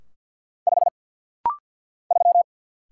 Morse code, heard as a keyed tone, about 50 wpm, 700 Hz. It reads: 5 3
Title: Single Letter-Number - Rapid-Fire 50wpm